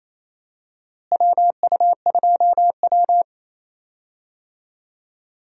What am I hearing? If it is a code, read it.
Morse code, heard as a keyed tone, about 27 wpm, 700 Hz. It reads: WU2W
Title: USA Call Signs - 4 Characters - Repeated 3x - 28wpm